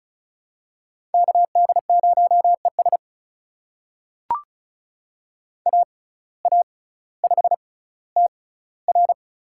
Morse code, heard as a keyed tone, about 36 wpm, spaced as for 18 wpm, 700 Hz. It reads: KD0ES AA5TR
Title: USA Call Signs - 5 Characters - 6x Character Spacing - 35wpm